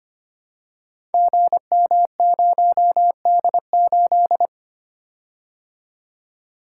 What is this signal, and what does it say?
Morse code, heard as a keyed tone, about 25 wpm, 690 Hz. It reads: GM0D8
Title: Code Groups - 5 Letter:Numbers - 25wpm